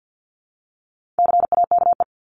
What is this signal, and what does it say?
Morse code, heard as a keyed tone, about 50 wpm, 700 Hz. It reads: CAKE
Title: Top 1501-1750 Words 50wpm